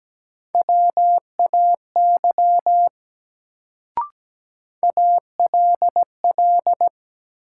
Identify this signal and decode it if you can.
Morse code, heard as a keyed tone, about 17 wpm, 700 Hz. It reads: WAY ALL
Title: Sentences from Top 100 Words - New Word Drill - 17wpm